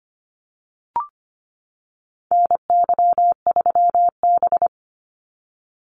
Morse code, heard as a keyed tone, about 25 wpm, 700 Hz. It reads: NY3B
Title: Call Signs - 25wpm